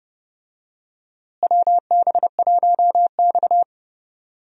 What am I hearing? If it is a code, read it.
Morse code, heard as a keyed tone, about 30 wpm, 700 Hz. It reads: WB1X
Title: USA Call Signs - 4 Characters - Repeated 2x - 30wpm